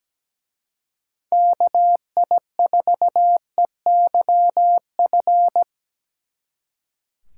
Morse code, heard as a keyed tone, about 20 wpm, 700 Hz. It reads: KI4EYF
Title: Speed Racing - USA 2x3 Call Signs 17wpm